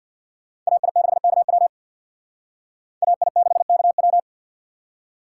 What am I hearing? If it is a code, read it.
Morse code, heard as a keyed tone, about 61 wpm, 700 Hz. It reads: AI6KW AI6KW
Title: Speed Racing - USA 2x2 Call Signs 50wpm